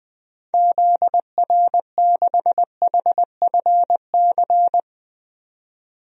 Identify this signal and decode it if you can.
Morse code, heard as a keyed tone, about 20 wpm, 700 Hz. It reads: ZR6HFC
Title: DXCC-100 Call Signs - 6 Characters - 20wpm